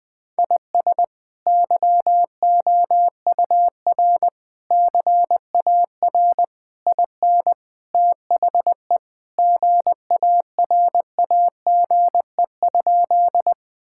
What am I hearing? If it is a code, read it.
Morse code, heard as a keyed tone, about 21 wpm, 700 Hz. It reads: IS YOUR CAR IN THE GARAGE?
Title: Sentences from Top 1600 Words 20wpm